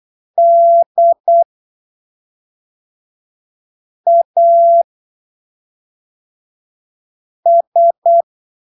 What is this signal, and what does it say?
Morse code, heard as a keyed tone, about 8 wpm, spaced as for 3 wpm, 680 Hz. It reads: DAS
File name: arquivos / texto / texto 10 08ppm